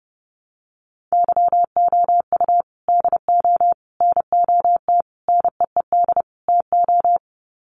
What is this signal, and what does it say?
Morse code, heard as a keyed tone, about 28 wpm, 700 Hz. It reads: YOU DO NOT NEED TO
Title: Sets of 5 Words 30wpm